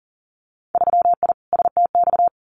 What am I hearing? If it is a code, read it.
Morse code, heard as a keyed tone, about 43 wpm, 700 Hz. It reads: 3I STX
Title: Winter Field Day Exchanges Encore 40wpm